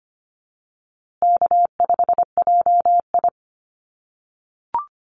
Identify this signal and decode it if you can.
Morse code, heard as a keyed tone, about 25 wpm, 700 Hz. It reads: K5JI
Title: USA Call Signs Repeated 3x 25wpm